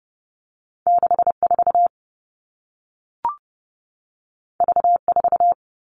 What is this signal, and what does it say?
Morse code, heard as a keyed tone, about 30 wpm, 700 Hz. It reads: 6 4 V 4
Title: All Cut Numbers - 2-Digit 30wpm